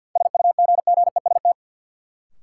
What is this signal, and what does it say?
Morse code, heard as a keyed tone, about 53 wpm, 700 Hz. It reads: SUGGEST
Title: Sentences from Top 1300 Words - New Word Drill Alt - 50wpm